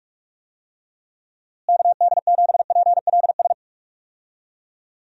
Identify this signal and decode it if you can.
Morse code, heard as a keyed tone, about 44 wpm, 700 Hz. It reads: KD7PLS
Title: USA Call Signs - 6 Characters - Repeated 2x - 45wpm